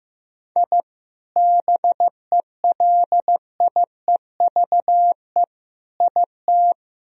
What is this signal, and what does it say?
Morse code, heard as a keyed tone, about 15 wpm, 700 Hz. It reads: I BELIEVE IT
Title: Sentences from Top 1000 Words - Review Part 1 - 15wpm